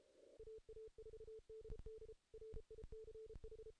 Morse code, heard as a keyed tone, about 32 wpm, 440 Hz. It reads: AA4DD RICH
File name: 2016 JAN CW NAQP / 2016 Jan naqp k5ot - 0-8